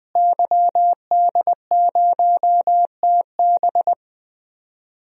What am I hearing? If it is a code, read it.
Morse code, heard as a keyed tone, about 21 wpm, 700 Hz. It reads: YD0TB
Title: DXCC-100 Call Signs - 5 Characters - Repeated 2x - 20wpm